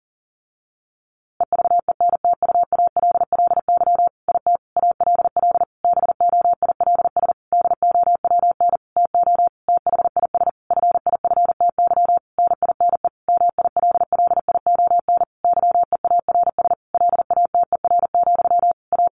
Morse code, heard as a keyed tone, about 40 wpm, 700 Hz. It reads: EVENTUALLY IT ALL BOILS DOWN TO THIS FIFTY NINE MILLION YEARS LATER, A
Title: A Sound Of Thunder 40wpm